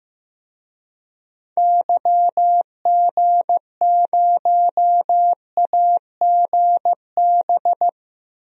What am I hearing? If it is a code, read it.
Morse code, heard as a keyed tone, about 15 wpm, 700 Hz. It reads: YG0AGB
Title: DXCC-100 Call Signs - 6 Characters - 15wpm